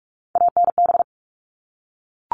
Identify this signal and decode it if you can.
Morse code, heard as a keyed tone, about 44 wpm, 700 Hz. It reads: ANB
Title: All Cut Numbers - 3-Digit 45wpm